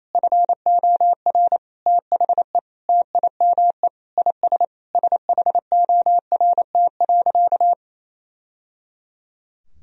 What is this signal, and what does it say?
Morse code, heard as a keyed tone, about 28 wpm, 700 Hz. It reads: FOR THE TIME IS SHORT.
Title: Sentences from Top 1000 Words - Review Part 2 - 28wpm